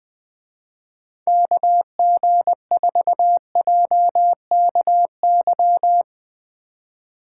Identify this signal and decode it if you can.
Morse code, heard as a keyed tone, about 20 wpm, 700 Hz. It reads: KG4JKY
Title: Speed Racing - USA 2x3 Call Signs 17wpm